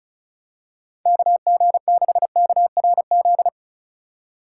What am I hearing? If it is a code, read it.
Morse code, heard as a keyed tone, about 36 wpm, 700 Hz. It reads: KG6KRZ
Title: USA Call Signs - 6 Characters - 3x Character Spacing - 35wpm